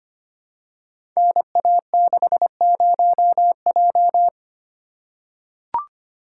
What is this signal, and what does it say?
Morse code, heard as a keyed tone, about 25 wpm, 690 Hz. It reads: NA60J
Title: Code Groups - 5 Letter:Numbers - 25wpm